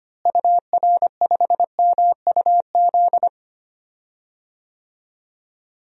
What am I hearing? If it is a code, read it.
Morse code, heard as a keyed tone, about 25 wpm, 700 Hz. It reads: UR5MUZ
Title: DXCC-100 Call Signs - 6 Characters - Repeated 2x - 25wpm